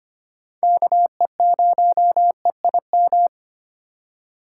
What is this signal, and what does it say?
Morse code, heard as a keyed tone, about 25 wpm, 700 Hz. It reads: KE0EIM
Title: USA Call Signs - 6 Characters - 4x Character Spacing - 25wpm